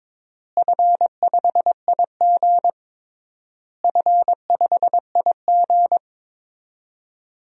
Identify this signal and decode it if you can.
Morse code, heard as a keyed tone, about 22 wpm, 700 Hz. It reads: F5IG F5IG
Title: DXCC-100 Call Signs - 4 Characters - Repeated 3x - 22wpm